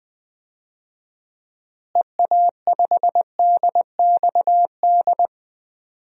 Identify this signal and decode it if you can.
Morse code, heard as a keyed tone, about 20 wpm, 700 Hz. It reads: EA5DXD
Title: DXCC-100 Call Signs - 6 Characters - Repeated 3x - 20wpm